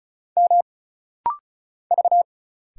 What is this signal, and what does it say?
Morse code, heard as a keyed tone, about 34 wpm, 700 Hz. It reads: M V
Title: Single Letter-Number - Rapid-Fire 35wpm